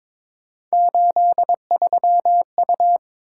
Morse code, heard as a keyed tone, about 22 wpm, 700 Hz. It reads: 83U
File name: practice / 22wpm / Three Letter-Number Combinations - 22wpm